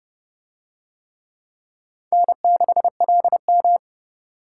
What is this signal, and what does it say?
Morse code, heard as a keyed tone, about 29 wpm, 700 Hz. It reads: N6LM